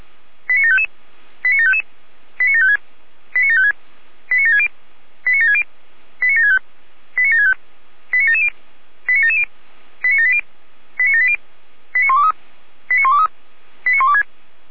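Five-tone selective call (ZVEI), heard as quick runs of five tones, tones of about 70 ms. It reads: 89755 89755 89756 89756 89760 89760 89765 89765 89770 89770 89780 89780 89123 89123 89126
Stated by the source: ZVEI five-tone selective call 89752 received by SDR.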